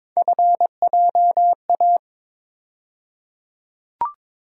Morse code, heard as a keyed tone, about 22 wpm, 700 Hz. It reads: FJA
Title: USA Call Sign Postfixes - 3 Letters 22wpm